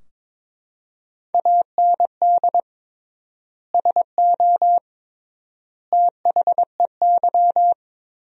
Morse code, heard as a keyed tone, about 22 wpm, 700 Hz. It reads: AND SO THEY
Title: 3x Word-Spacing - Sentences from Top 400 Words - 22wpm